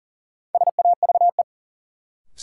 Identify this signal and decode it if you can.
Morse code, heard as a keyed tone, about 40 wpm, 700 Hz. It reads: SAVE
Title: Sentences from Top 1100 Words - New Word Drill Alt - 40wpm